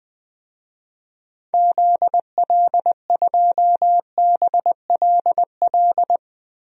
Morse code, heard as a keyed tone, about 20 wpm, 700 Hz. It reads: ZL2BLL
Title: DXCC-100 Call Signs - 6 Characters - 20wpm